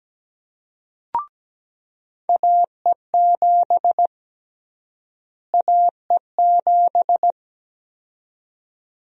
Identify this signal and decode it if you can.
Morse code, heard as a keyed tone, about 17 wpm, 700 Hz. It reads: AE7 AE7
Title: USA Call Sign Prefixes - Repeated 2x - 17wpm